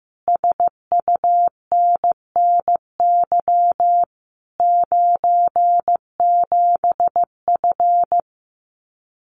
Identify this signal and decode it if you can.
Morse code, heard as a keyed tone, about 15 wpm, 700 Hz. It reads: SUNNY 97F
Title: Common QSO Phrases Encore 15wpm